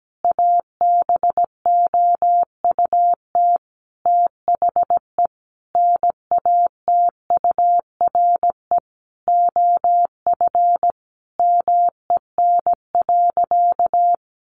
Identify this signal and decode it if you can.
Morse code, heard as a keyed tone, about 17 wpm, 700 Hz. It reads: ABOUT THE NATURE OF MEN.
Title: Sentences from Top 1500 Words 17wpm